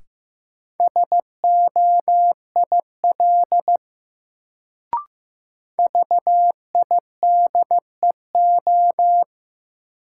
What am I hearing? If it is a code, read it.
Morse code, heard as a keyed tone, about 15 wpm, 700 Hz. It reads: SOIL VIDEO
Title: Sentences from Top 1000 Words - Review Part 4 - New Word Drill - 15wpm